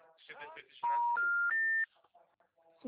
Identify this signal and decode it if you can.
Special information tone.